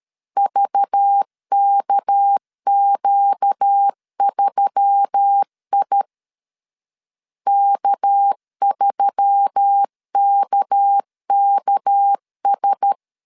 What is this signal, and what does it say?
Morse code, heard as a keyed tone, about 13 wpm, 800 Hz. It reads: VKQ3I K3KKS
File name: LEZIONI / 6 PKX0 / 6 PK 12wpm 2006